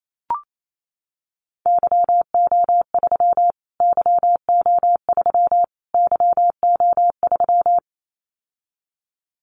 Morse code, heard as a keyed tone, about 28 wpm, 700 Hz. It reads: YO3 YO3 YO3
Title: Callsign Prefixes for Top 25 DXCC - Repeated 3x - 28wpm